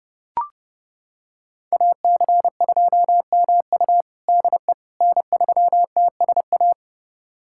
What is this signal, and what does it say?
Morse code, heard as a keyed tone, about 30 wpm, 700 Hz. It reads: AC2MU DE N3TSA
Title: USA Callsign DE Callsign - 5 Characters - 30wpm